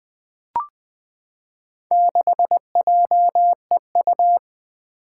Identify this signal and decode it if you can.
Morse code, heard as a keyed tone, about 20 wpm, 700 Hz. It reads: 6JEU
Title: Four Challenging Letter-Number Combinations 20wpm